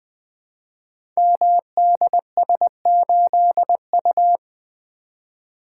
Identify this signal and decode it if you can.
Morse code, heard as a keyed tone, about 20 wpm, 700 Hz. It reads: MDS8U
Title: Code Groups - 5 Letter:Numbers - 20wpm